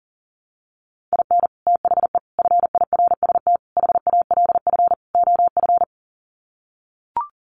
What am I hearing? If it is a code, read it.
Morse code, heard as a keyed tone, about 40 wpm, 700 Hz. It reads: IN THE FIRST HALF OF
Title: Sets of 5 Words 40wpm